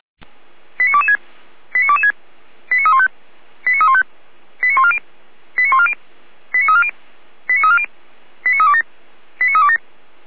Five-tone selective call (ZVEI), heard as quick runs of five tones, tones of about 70 ms. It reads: 89226 89226 89315 89315 89140 89140 89340 89340 89327 89327